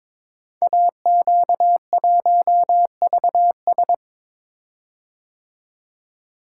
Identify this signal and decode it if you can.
Morse code, heard as a keyed tone, about 22 wpm, 700 Hz. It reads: AQ1VS
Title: Code Groups - 5 Letter:Numbers - 22wpm